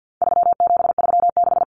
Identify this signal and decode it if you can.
Morse code, heard as a keyed tone, about 48 wpm, 700 Hz. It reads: 3736
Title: Numbers - 4-Digit 50wpm